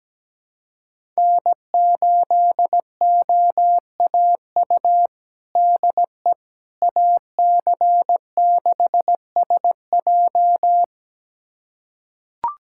Morse code, heard as a keyed tone, about 17 wpm, 700 Hz. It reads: N8OAU DE AC6SJ
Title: USA Callsign DE Callsign - 5 Characters - 17wpm